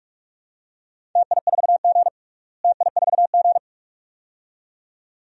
Morse code, heard as a keyed tone, about 42 wpm, 700 Hz. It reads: TI4G TI4G
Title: DXCC-100 Call Signs - 4 Characters - Repeated 2x - 45wpm